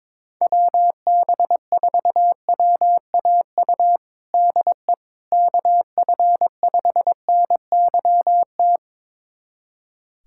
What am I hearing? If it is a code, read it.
Morse code, heard as a keyed tone, about 21 wpm, 700 Hz. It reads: WB4WAU DE KF5NYT
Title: USA Callsign DE Callsign - 6 Characters - 22wpm